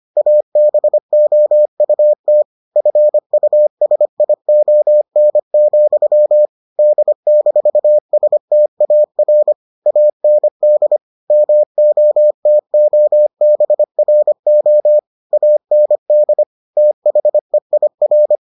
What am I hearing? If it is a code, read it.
Morse code, heard as a keyed tone, about 25 wpm, 600 Hz. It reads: ABOUT FUSION, D-STAR AND MOTOBRO AND THEIR